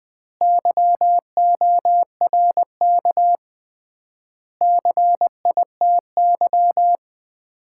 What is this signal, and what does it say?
Morse code, heard as a keyed tone, about 20 wpm, 700 Hz. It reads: YORK CITY
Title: 3x Word-Spacing - Sets of 4 Words - 20wpm